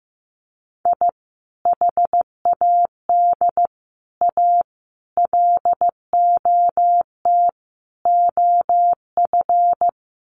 Morse code, heard as a keyed tone, about 15 wpm, 700 Hz. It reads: I HAD A LOT OF
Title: Sets of 5 Words Encore 15wpm